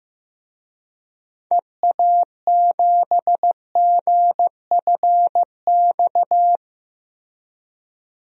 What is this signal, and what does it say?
Morse code, heard as a keyed tone, about 15 wpm, 700 Hz. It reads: EA7GFX
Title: DXCC-100 Call Signs - 6 Characters - 15wpm